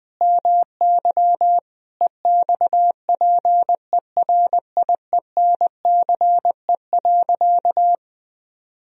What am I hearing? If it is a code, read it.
Morse code, heard as a keyed tone, about 20 wpm, 700 Hz. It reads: MY EXPERIENCE.
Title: Sentences from Top 1000 Words - Review Part 3 - 20wpm